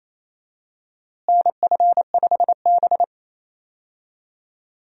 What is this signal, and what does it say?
Morse code, heard as a keyed tone, about 28 wpm, 700 Hz. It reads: NF5B
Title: USA Call Signs - 4 Characters - Repeated 2x - 28wpm